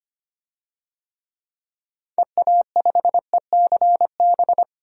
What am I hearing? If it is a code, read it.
Morse code, heard as a keyed tone, about 25 wpm, 700 Hz. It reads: EA5ECB
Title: DXCC-100 Call Signs - 6 Characters - 25wpm